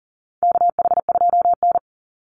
Morse code, heard as a keyed tone, about 40 wpm, 700 Hz. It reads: KH2N
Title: Call Signs Repeated 3x - 40wpm